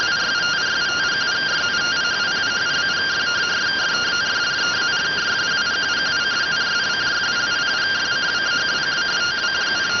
Two-tone teleprinter signal, about 170 Hz shift.